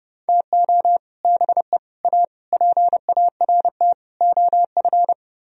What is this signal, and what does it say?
Morse code, heard as a keyed tone, about 33 wpm, 700 Hz. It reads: TO BE A PART OF